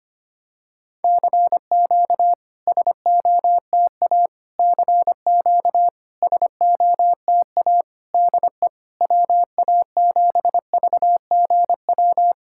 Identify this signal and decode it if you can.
Morse code, heard as a keyed tone, about 25 wpm, 700 Hz. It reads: CQ SOTA CQ SOTA DE WA7VGW
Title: SOTA QSO Practice 25wpm